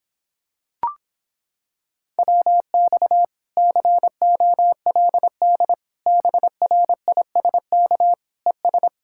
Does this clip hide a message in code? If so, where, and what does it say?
Yes: Morse code from 2.2 to 8.9 s, heard as a keyed tone, about 26 wpm, 700 Hz. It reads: WX COLD BRISK ES